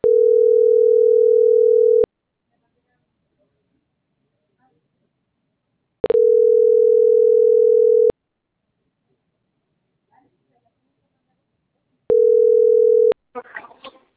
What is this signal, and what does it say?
Ringback tone (North America), 440 + 480 Hz.